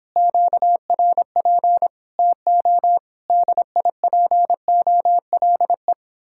Morse code, heard as a keyed tone, about 26 wpm, 700 Hz. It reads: QRP TO DIPOLE